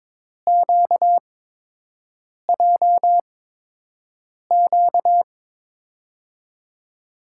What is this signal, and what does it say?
Morse code, heard as a keyed tone, about 22 wpm, spaced as for 6 wpm, 700 Hz. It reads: QJQ
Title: USA Call Signs - 6 Characters - 8x Character Spacing - 22wpm